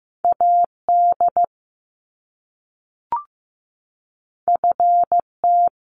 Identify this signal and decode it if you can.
Morse code, heard as a keyed tone, about 15 wpm, 700 Hz. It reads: AD FT
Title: Top Two Letter Combinations in Words 15wpm